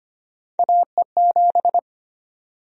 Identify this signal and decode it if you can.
Morse code, heard as a keyed tone, about 25 wpm, 700 Hz. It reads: AE7
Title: USA Call Sign Prefixes - 25wpm